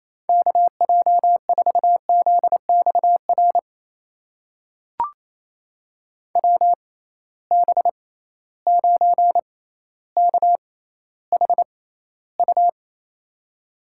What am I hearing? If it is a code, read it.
Morse code, heard as a keyed tone, about 27 wpm, spaced as for 14 wpm, 700 Hz. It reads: KJ4ZXR WB9KHU
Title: USA Call Signs - 6 Characters - 6x Character Spacing - 28wpm